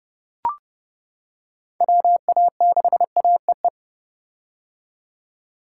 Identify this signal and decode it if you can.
Morse code, heard as a keyed tone, about 30 wpm, 700 Hz. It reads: WA6AEE
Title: K1USN SST QSO Practice - 30wpm